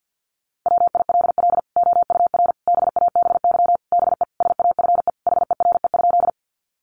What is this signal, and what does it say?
Morse code, heard as a keyed tone, about 50 wpm, 700 Hz. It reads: WILL OUR BABY BE SAFE HERE?
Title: Sentences from Top 1300 Words 50wpm